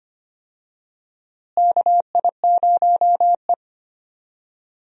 Morse code, heard as a keyed tone, about 25 wpm, 700 Hz. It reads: KI0E
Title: USA Call Signs - 4 Characters - Repeated 3x - 25wpm